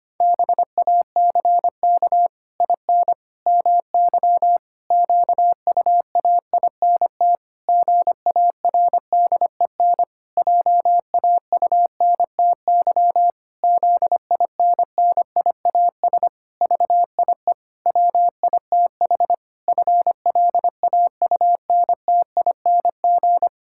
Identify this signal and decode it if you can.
Morse code, heard as a keyed tone, about 25 wpm, 700 Hz. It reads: BACK IN MY QUAINT GARDEN JAUNTY ZINNIAS VIE WITH FLAUNTING
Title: Pangram Challenge v2 - 25wpm